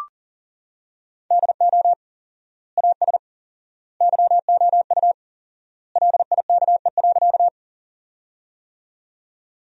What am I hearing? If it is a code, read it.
Morse code, heard as a keyed tone, about 37 wpm, 700 Hz. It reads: DO AS YOU LIKE.
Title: 4x Word-Spacing - Sentences from Top 100 Words - 40wpm